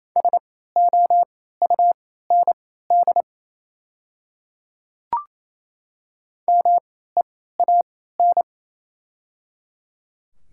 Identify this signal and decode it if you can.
Morse code, heard as a keyed tone, about 27 wpm, spaced as for 16 wpm, 700 Hz. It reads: SOUND MEAN
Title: 3x Character Spacing - Top 300 Words - 28wpm